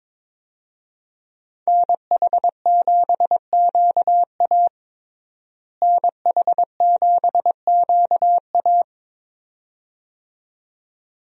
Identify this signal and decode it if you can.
Morse code, heard as a keyed tone, about 23 wpm, 700 Hz. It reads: NH7QA NH7QA
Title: USA Call Signs - 5 Characters - Repeated 2x - 22wpm